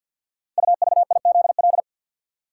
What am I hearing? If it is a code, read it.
Morse code, heard as a keyed tone, about 47 wpm, 700 Hz. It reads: UVIZL
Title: Code Groups - 5 Letters - 50wpm